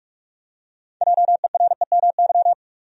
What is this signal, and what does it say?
Morse code, heard as a keyed tone, about 43 wpm, 700 Hz. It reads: JEREMY
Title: USA Common Names - 45wpm